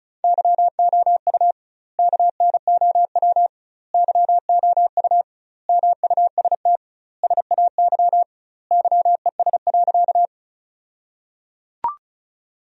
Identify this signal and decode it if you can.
Morse code, heard as a keyed tone, about 36 wpm, 700 Hz. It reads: YOU KNOW YOU MUST SAY YES.
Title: 2x Word-Spacing - Sentences from Top 500 Words - 35wpm